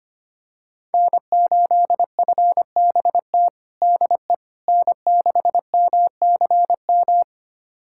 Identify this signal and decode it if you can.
Morse code, heard as a keyed tone, about 25 wpm, 700 Hz. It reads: N8FBT DE N6MCM
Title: USA Callsign DE Callsign - 5 Characters - 25wpm